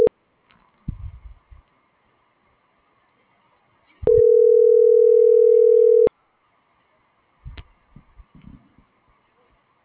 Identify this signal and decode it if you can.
Ringback tone (North America), 440 + 480 Hz.